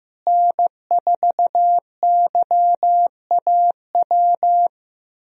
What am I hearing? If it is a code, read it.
Morse code, heard as a keyed tone, about 15 wpm, 700 Hz. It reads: N4YAW
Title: USA Call Signs - 5 Characters - Repeated 3x - 15wpm